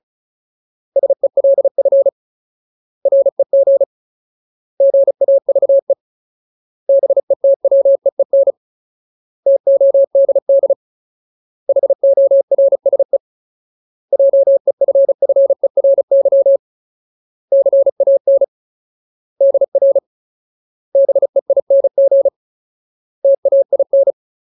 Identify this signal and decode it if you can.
Morse code, heard as a keyed tone, about 36 wpm, 550 Hz. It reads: SELF REG GAVE BETWEEN TODD HORSE JEFFERY CAN DR BEING TAIN
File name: Ultimate Word List / 35WPM Ultimate